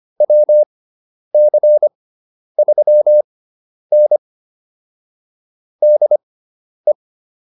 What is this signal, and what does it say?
Morse code, heard as a keyed tone, about 25 wpm, spaced as for 10 wpm, 600 Hz. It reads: WC3N DE